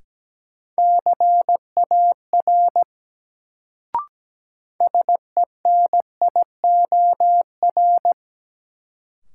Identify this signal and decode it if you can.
Morse code, heard as a keyed tone, about 17 wpm, 700 Hz. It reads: CAR SENIOR
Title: Sets of 5 Words Encore - New Word Drill - 17wpm